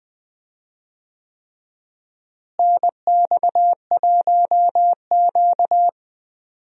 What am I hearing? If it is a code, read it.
Morse code, heard as a keyed tone, about 20 wpm, 700 Hz. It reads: NX1Q
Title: USA Call Signs - 4 Characters - Repeated 3x - 20wpm